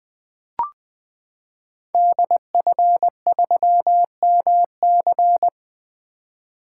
Morse code, heard as a keyed tone, about 20 wpm, 700 Hz. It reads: DF3MC
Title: 2023 SOTA Activator Call Signs - 20wpm